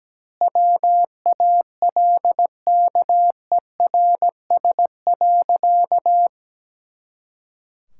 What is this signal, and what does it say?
Morse code, heard as a keyed tone, about 17 wpm, 700 Hz. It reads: WALKERS.
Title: Pangram Challenge v2 - 17wpm